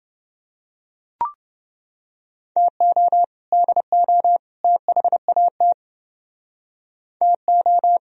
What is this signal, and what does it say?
Morse code, heard as a keyed tone, about 29 wpm, 700 Hz. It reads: TO DO THAT TO